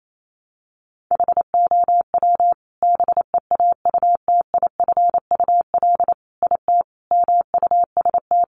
Morse code, heard as a keyed tone, about 28 wpm, 700 Hz. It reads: HOW BEAUTIFUL IT MUST
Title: Sentences from Top 900 Words 28wpm